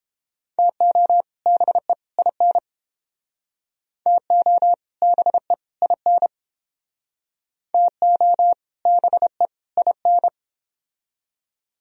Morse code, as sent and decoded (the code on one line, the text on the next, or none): - --- / -... / .. -. / - --- / -... / .. -. / - --- / -... / .. -.
TO BE IN TO BE IN TO BE IN